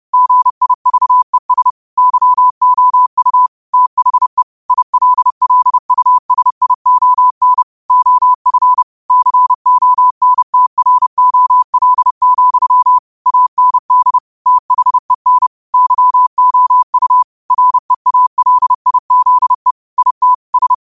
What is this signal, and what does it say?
Morse code, heard as a keyed tone, about 30 wpm, 1000 Hz. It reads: GIVES YOU THE ILLUSION OF CONTROL, AND THEN YOU REALIZE IT S